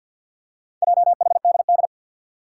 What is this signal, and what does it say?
Morse code, heard as a keyed tone, about 47 wpm, 700 Hz. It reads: JHDD
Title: Four Challenging Letter-Number Combinations 50wpm